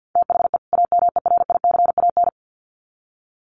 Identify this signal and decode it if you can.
Morse code, heard as a keyed tone, about 50 wpm, 700 Hz. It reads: THE AMERICAN
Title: Sets of 2 Words 50wpm